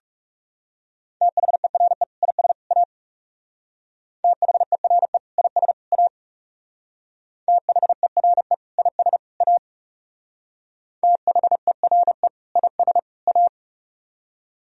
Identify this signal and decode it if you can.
Morse code, heard as a keyed tone, about 36 wpm, 700 Hz. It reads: THERE IS A THERE IS A THERE IS A THERE IS A
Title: Speed-Racing - Sets of 3 Words 30wpm